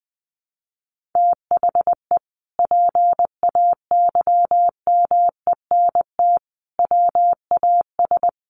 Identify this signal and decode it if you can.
Morse code, heard as a keyed tone, about 20 wpm, 700 Hz. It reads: THE PAYMENT WAS
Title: Sentences from Top 1200 Words 20wpm